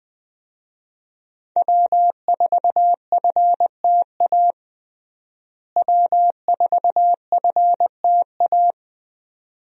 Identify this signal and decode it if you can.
Morse code, heard as a keyed tone, about 20 wpm, 700 Hz. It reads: W4FTA W4FTA
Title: USA Call Signs - 5 Characters - Repeated 2x - 20wpm